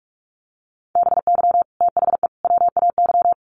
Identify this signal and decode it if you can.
Morse code, heard as a keyed tone, about 44 wpm, 700 Hz. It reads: BY THE WAY
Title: Sets of 3 Words 45wpm